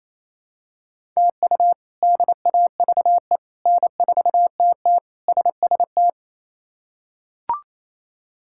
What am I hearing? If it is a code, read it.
Morse code, heard as a keyed tone, about 28 wpm, 700 Hz. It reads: TU DAVE N4TT SST
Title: K1USN SST QSO Practice - 28wpm